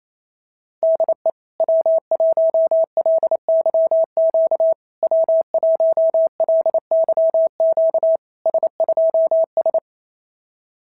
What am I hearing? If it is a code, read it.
Morse code, heard as a keyed tone, about 28 wpm, 650 Hz. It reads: DE W1LYQ W1LYQ S2S
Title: SOTA QSO Practice 28wpm